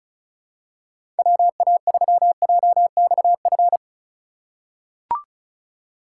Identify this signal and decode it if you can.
Morse code, heard as a keyed tone, about 34 wpm, 700 Hz. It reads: WA3JXF
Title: USA Call Signs - 6 Characters - 3x Character Spacing - 35wpm